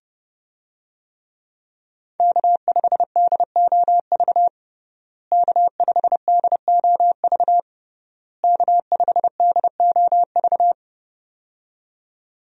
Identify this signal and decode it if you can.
Morse code, heard as a keyed tone, about 32 wpm, 700 Hz. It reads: K5DOV K5DOV K5DOV
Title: USA Call Signs - 5 Characters - Repeated 3x - 30wpm